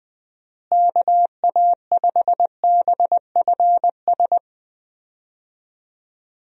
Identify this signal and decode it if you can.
Morse code, heard as a keyed tone, about 20 wpm, 700 Hz. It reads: KA5BFS
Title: USA Call Signs - 6 Characters - Repeated 2x - 20wpm